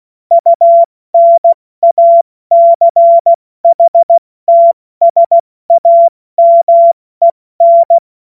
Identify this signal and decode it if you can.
Morse code, heard as a keyed tone, about 16 wpm, 680 Hz. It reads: UNACHTSAMEN
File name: Morsekurs / Witze / Witz 03 Juni Mai 2025 70 F80